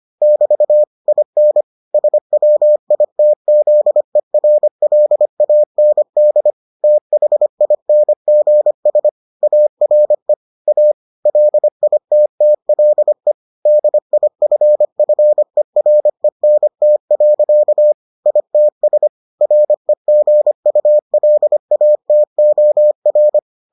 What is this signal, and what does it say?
Morse code, heard as a keyed tone, about 25 wpm, 600 Hz. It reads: <BT> IN SWITZERLAND THINGS ARE A LITTLE DIFFERENT. ITS REGULATOR